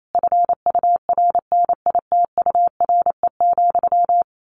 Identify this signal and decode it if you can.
Morse code, heard as a keyed tone, about 28 wpm, 700 Hz. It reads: FURNITURE,
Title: A Sound Of Thunder 28wpm